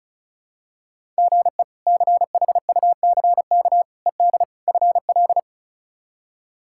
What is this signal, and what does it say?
Morse code, heard as a keyed tone, about 36 wpm, 700 Hz. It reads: GE CHUCK ED FL
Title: K1USN SST QSO Practice - 35wpm